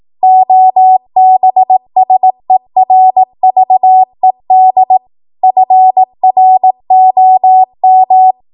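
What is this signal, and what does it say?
Morse code, heard as a keyed tone, about 18 wpm, 750 Hz. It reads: OBSERVED FROM
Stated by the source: 18 WPM Morse code practice from a national amateur radio association's code archive.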